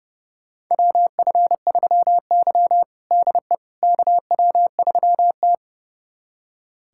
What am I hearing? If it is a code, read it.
Morse code, heard as a keyed tone, about 32 wpm, 700 Hz. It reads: WF3Y DE KW3T
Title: USA Callsign DE Callsign - 4 Characters - 30wpm